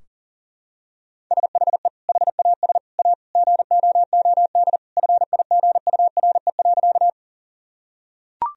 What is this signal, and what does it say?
Morse code, heard as a keyed tone, about 40 wpm, 700 Hz. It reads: SHE HAS A GOOD FIGURE.
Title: Sentences from Top 1000 Words - Review Part 2 - 40wpm